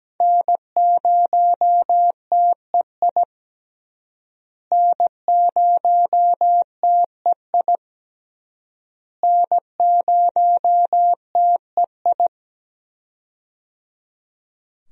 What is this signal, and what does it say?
Morse code, heard as a keyed tone, about 17 wpm, 700 Hz. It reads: N0TEI N0TEI N0TEI
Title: USA Call Signs - 5 Characters - Repeated 3x - 17wpm